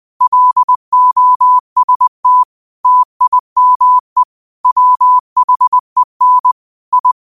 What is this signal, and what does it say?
Morse code, heard as a keyed tone, about 20 wpm, 1000 Hz. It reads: LOST TIME WHEN I